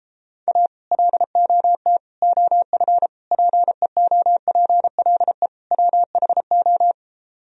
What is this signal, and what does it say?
Morse code, heard as a keyed tone, about 33 wpm, 700 Hz. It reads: A LOT OF PEOPLE WHO